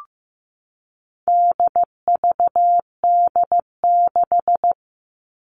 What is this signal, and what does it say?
Morse code, heard as a keyed tone, about 15 wpm, 700 Hz. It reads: DVD6
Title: All Cut Numbers - 4-Digit 15wpm